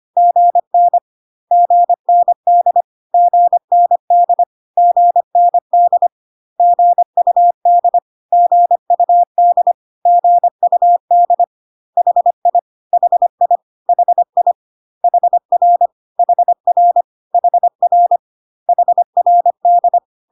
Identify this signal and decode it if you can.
Morse code, heard as a keyed tone, about 25 wpm, 700 Hz. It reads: GN GND GND GND GUD GUD GUD HI HI HI HR HR HR HRD
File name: cw / CW common abbreviations alphabetical repeated 3 times 25 25wpm ik1hge